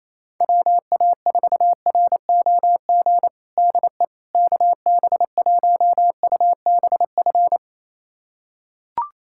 Morse code, heard as a keyed tone, about 27 wpm, 700 Hz. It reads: WA4ROG DE KB1UBF
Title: USA Callsign DE Callsign - 6 Characters - 28wpm